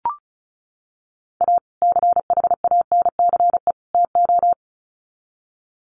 Morse code, heard as a keyed tone, about 34 wpm, 700 Hz. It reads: A CHANCE TO
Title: Sets of 3 Words 35wpm